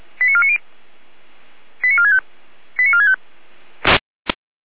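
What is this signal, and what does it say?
Five-tone selective call (ZVEI), heard as quick runs of five tones, tones of about 70 ms. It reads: 89490 89465 89465
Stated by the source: ZVEI five-tone selective call 89581 received on an SDR.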